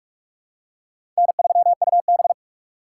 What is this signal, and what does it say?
Morse code, heard as a keyed tone, about 46 wpm, 700 Hz. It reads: N3UB
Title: USA Call Signs - 4 Characters - 2x Character Spacing - 45wpm